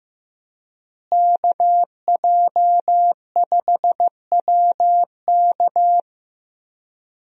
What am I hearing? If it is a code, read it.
Morse code, heard as a keyed tone, about 15 wpm, 700 Hz. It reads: KJ5WK
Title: USA Call Signs - 5 Characters - Repeated 2x - 15wpm